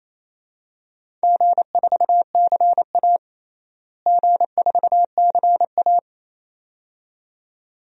Morse code, heard as a keyed tone, about 27 wpm, 700 Hz. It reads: G4CA G4CA
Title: DXCC-100 Call Signs - 4 Characters - Repeated 2x - 28wpm